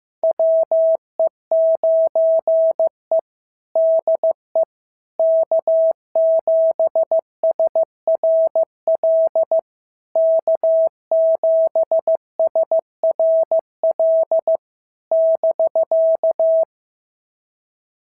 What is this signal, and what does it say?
Morse code, heard as a keyed tone, about 14 wpm, 650 Hz. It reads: WE9E DE K7SRL K7SRL <BK>
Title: SOTA QSO Practice 15wpm